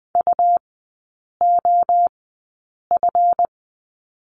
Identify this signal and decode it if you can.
Morse code, heard as a keyed tone, about 20 wpm, spaced as for 8 wpm, 690 Hz. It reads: UOF
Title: 145 Lesson 24e - Letter y - Three Characters 20wpm